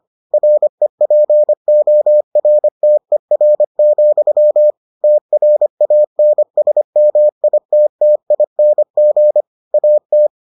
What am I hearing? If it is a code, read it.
Morse code, heard as a keyed tone, about 25 wpm, 600 Hz. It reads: REPORTER, TRANSMITTING AT